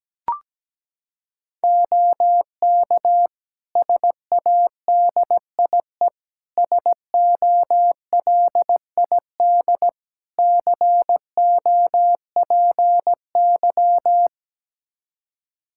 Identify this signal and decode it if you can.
Morse code, heard as a keyed tone, about 17 wpm, 700 Hz. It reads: OK SADIE SOLID COPY
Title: Practice QSO - Exchange 3 - More Info - 17wpm